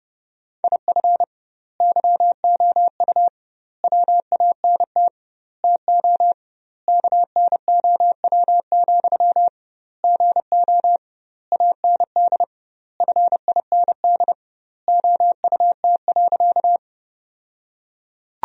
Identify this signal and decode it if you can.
Morse code, heard as a keyed tone, about 32 wpm, 700 Hz. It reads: IF YOU WANT TO KNOW, GO AND FIND OUT.
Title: 2x Word-Spacing - Sentences from Top 200 Words - 30wpm